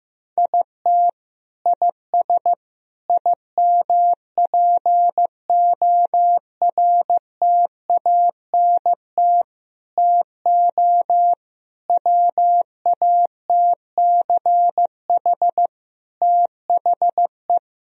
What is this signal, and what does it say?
Morse code, heard as a keyed tone, about 15 wpm, 700 Hz. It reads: IT IS IMPORTANT TO WATCH THE
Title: Sentences from Top 1000 Words - Review Part 4 - 15wpm